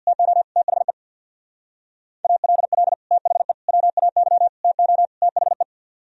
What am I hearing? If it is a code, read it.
Morse code, heard as a keyed tone, about 64 wpm, 700 Hz. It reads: TO THE ALL THE WAY TO THE